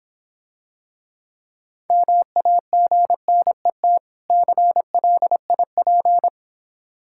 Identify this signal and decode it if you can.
Morse code, heard as a keyed tone, about 26 wpm, 700 Hz. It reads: MAGNET CLIP